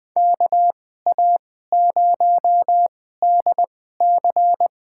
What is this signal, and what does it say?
Morse code, heard as a keyed tone, about 20 wpm, spaced as for 15 wpm, 700 Hz. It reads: KA0DC
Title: USA Call Signs - 2x Character Spacing - 20wpm